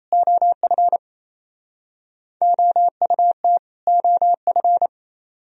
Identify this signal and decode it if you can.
Morse code, heard as a keyed tone, about 37 wpm, 700 Hz. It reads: OF OUT OF